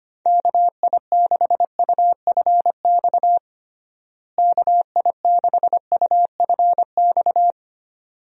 Morse code, heard as a keyed tone, about 25 wpm, 700 Hz. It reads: KI6UFX KI6UFX
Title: USA Call Signs - 6 Characters - Repeated 2x - 25wpm